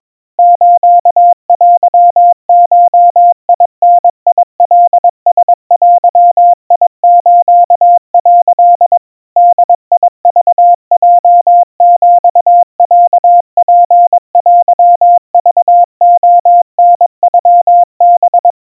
Morse code (Japanese Wabun code), heard as a keyed tone, about 22 wpm, 700 Hz. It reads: ステゴダカラデス」 ボクヲヒロツテクレタノハ
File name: 80 cpm 700 Hz Crispy